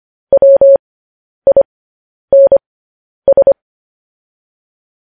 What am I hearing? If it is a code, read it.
Morse code, heard as a keyed tone, about 25 wpm, spaced as for 10 wpm, 550 Hz. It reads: WINS